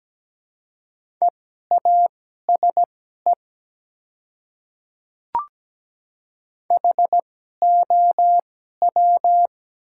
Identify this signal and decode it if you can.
Morse code, heard as a keyed tone, about 17 wpm, spaced as for 12 wpm, 700 Hz. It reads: EASE HOW
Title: 2x Character Spacing - Top 300 Words - 17wpm